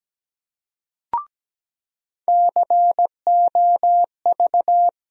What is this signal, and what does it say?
Morse code, heard as a keyed tone, about 17 wpm, 700 Hz. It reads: COV
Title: USA Call Sign Postfixes - 3 Letters 17wpm